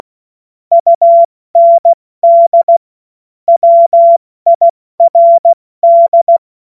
Morse code, heard as a keyed tone, about 16 wpm, 680 Hz. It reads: UND WIRD
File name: Morsekurs / Witze / Witz 02 April 2024 70 F80